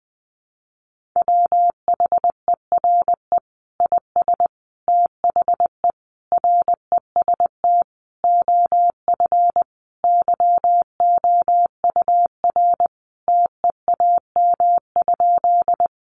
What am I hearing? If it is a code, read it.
Morse code, heard as a keyed tone, about 20 wpm, 700 Hz. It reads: WHERE IS THE REST OF YOUR TEAM?
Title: Sentences from Top 1500 Words 20wpm